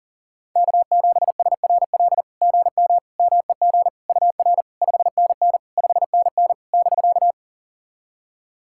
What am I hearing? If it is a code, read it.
Morse code, heard as a keyed tone, about 40 wpm, 700 Hz. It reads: K7SRL GM MEG UR 5NN 5NN <BK>
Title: SOTA QSO Practice 40wpm